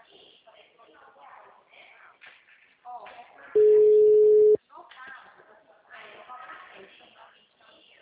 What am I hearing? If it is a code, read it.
Ringback tone (Europe, 425 Hz), with speech.